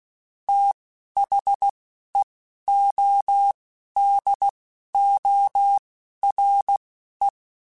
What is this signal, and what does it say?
Morse code, heard as a keyed tone, about 16 wpm, spaced as for 11 wpm, 800 Hz. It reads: THEODORE
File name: TH Runs / TH Runs mp3s / TH Runs @10